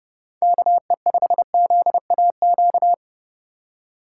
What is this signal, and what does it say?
Morse code, heard as a keyed tone, about 32 wpm, 700 Hz. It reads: KE5ZAQ
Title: USA Call Signs - 6 Characters - 3x Character Spacing - 30wpm